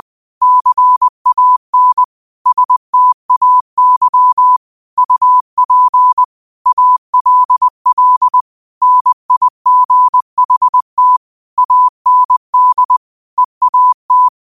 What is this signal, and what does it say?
Morse code, heard as a keyed tone, about 20 wpm, 1000 Hz. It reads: CAN STAY UP ALL NIGHT AND EAT
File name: qotdfiles / 20wpm